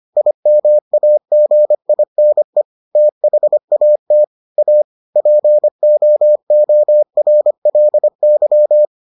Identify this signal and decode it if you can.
Morse code, heard as a keyed tone, about 25 wpm, 600 Hz. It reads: IMAGINE THAT A POORLY